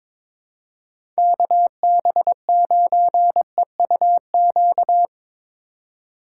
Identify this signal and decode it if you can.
Morse code, heard as a keyed tone, about 22 wpm, 700 Hz. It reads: KB9EUQ
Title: USA Call Signs - 2x Character Spacing - 22wpm